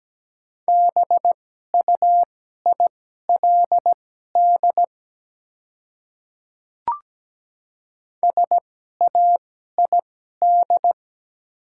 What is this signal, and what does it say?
Morse code, heard as a keyed tone, about 17 wpm, spaced as for 12 wpm, 700 Hz. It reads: BUILD SAID
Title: 2x Character Spacing - Top 200 Words - 17wpm